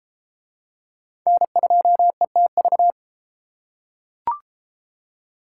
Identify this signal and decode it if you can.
Morse code, heard as a keyed tone, about 28 wpm, 700 Hz. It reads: N2ETV